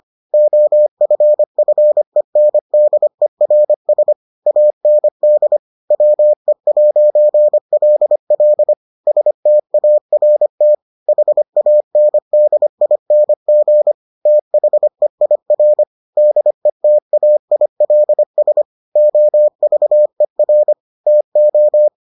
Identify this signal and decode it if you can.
Morse code, heard as a keyed tone, about 25 wpm, 600 Hz. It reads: OFFENDERS AND WE'LL START HANDING THEIR DETAILS OVER TO